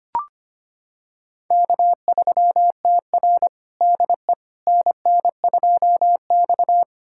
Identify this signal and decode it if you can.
Morse code, heard as a keyed tone, about 25 wpm, 700 Hz. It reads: K3TR DE NN2X
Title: USA Callsign DE Callsign - 4 Characters - 25wpm